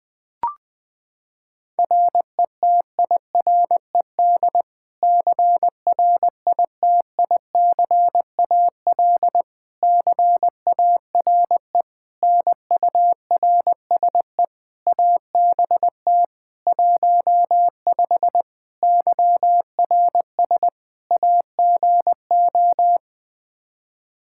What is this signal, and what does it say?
Morse code, heard as a keyed tone, about 20 wpm, 700 Hz. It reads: RETIRED CRITICAL CARE NURSE ABT 15 YRS AGO
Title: Practice QSO - Exchange 3 - More Info - 20wpm